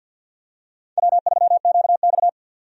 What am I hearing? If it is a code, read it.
Morse code, heard as a keyed tone, about 48 wpm, 700 Hz. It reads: W3QX
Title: USA Call Signs - 4 Characters - 50wpm